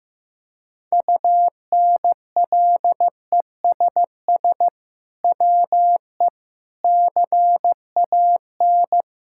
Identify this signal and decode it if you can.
Morse code, heard as a keyed tone, about 15 wpm, 700 Hz. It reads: UNLESS WE CAN
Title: Sentences from Top 1000 Words - Review Part 6 - 15wpm